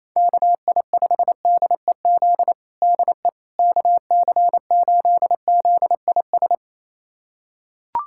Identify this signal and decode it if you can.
Morse code, heard as a keyed tone, about 28 wpm, 700 Hz. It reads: KI5DEZ DE KC8ZIS
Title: USA Callsign DE Callsign 28wpm